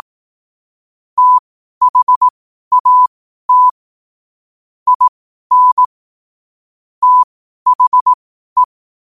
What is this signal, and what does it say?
Morse code, heard as a keyed tone, about 18 wpm, spaced as for 12 wpm, 1000 Hz. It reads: THAT IN THE
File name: qotdfiles / 12wpm